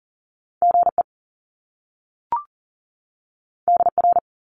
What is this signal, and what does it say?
Morse code, heard as a keyed tone, about 40 wpm, 700 Hz. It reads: GE DR
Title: Top Two Letter Combinations in Words 40wpm